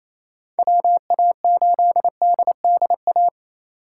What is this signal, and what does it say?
Morse code, heard as a keyed tone, about 28 wpm, 700 Hz. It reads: WA8DDA
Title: USA Call Signs - 8x Character Spacing - 28wpm